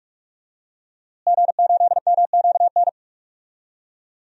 Morse code, heard as a keyed tone, about 41 wpm, 700 Hz. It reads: G8MQN
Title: DXCC-100 Call Signs - 5 Characters - Repeated 3x - 45wpm